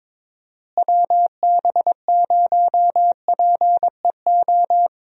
Morse code, heard as a keyed tone, about 23 wpm, 700 Hz. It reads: WB0PEO
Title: USA Call Signs - 6 Characters - Repeated 2x - 22wpm